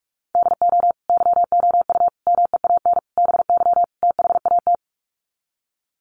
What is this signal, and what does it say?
Morse code, heard as a keyed tone, about 44 wpm, 700 Hz. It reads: DO YOU MEAN BY THAT
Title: Sets of 5 Words Encore2 45wpm